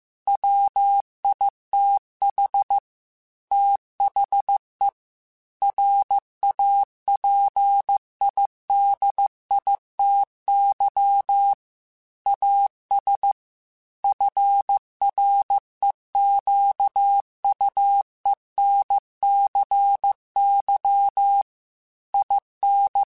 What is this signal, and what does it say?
Morse code, heard as a keyed tone, about 15 wpm, 800 Hz. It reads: WITH THE RAPIDITY AS FREQUENCY IN
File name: Technical Subjects / Technical Subjects-mp3s / XL @13